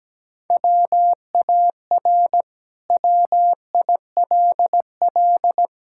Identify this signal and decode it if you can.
Morse code, heard as a keyed tone, about 17 wpm, 700 Hz. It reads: WAR WILL
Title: Sentences from Top 1000 Words - Review Part 6 - 17wpm